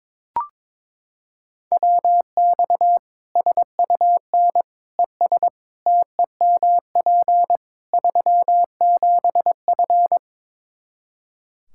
Morse code, heard as a keyed tone, about 22 wpm, 700 Hz. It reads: WX SUN ES TEMP 37F
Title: K1USN SST QSO Practice 22wpm